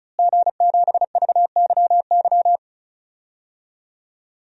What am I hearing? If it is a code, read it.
Morse code, heard as a keyed tone, about 36 wpm, 700 Hz. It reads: G7VYY
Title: DXCC-100 Call Signs - 5 Characters - Repeated 2x - 35wpm